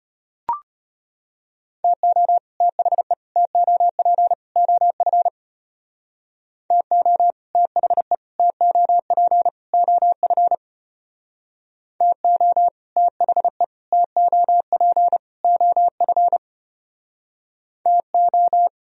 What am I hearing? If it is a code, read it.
Morse code, heard as a keyed tone, about 30 wpm, 700 Hz. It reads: TO THE TOP OF TO THE TOP OF TO THE TOP OF TO
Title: Speed-Racing - Sets of 4 Words 25wpm